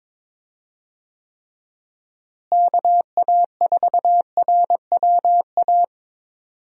Morse code, heard as a keyed tone, about 22 wpm, 700 Hz. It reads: KA4RWA
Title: USA Call Signs - 6 Characters - Repeated 2x - 22wpm